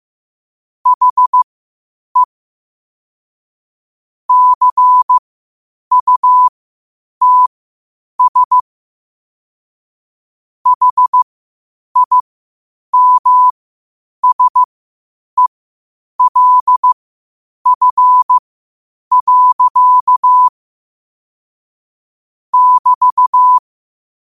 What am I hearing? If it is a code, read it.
Morse code, heard as a keyed tone, about 15 wpm, spaced as for 8 wpm, 1000 Hz. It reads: HE CUTS HIMSELF. <BT>
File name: qotdfiles / 8wpm